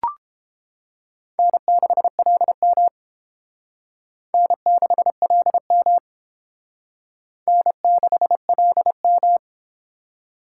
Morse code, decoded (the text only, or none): N6LM N6LM N6LM